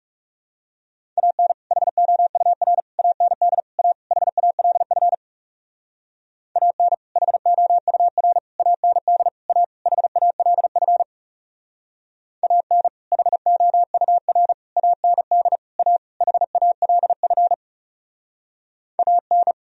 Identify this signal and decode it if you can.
Morse code, heard as a keyed tone, about 38 wpm, 700 Hz. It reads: AN HOUR AND A HALF AN HOUR AND A HALF AN HOUR AND A HALF AN